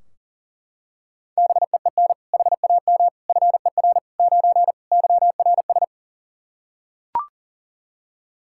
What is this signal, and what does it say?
Morse code, heard as a keyed tone, about 40 wpm, 700 Hz. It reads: BEEN HAM FER 9 YRS
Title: Practice QSO - Exchange 3 - More Info - 40wpm